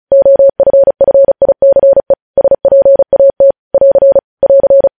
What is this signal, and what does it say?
Morse code, heard as a keyed tone, about 36 wpm, 550 Hz. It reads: OFFICE SPAT <AR> <AR>
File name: podcast / News headlines-1x-35x35